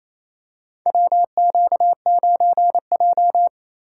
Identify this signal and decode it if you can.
Morse code, heard as a keyed tone, about 28 wpm, 700 Hz. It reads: WQ9J
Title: USA Call Signs - 4 Characters - 6x Character Spacing - 28wpm